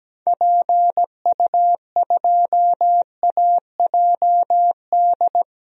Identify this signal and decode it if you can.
Morse code, heard as a keyed tone, about 17 wpm, 700 Hz. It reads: PU2AJD
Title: DXCC-100 Call Signs - 6 Characters - Repeated 3x - 17wpm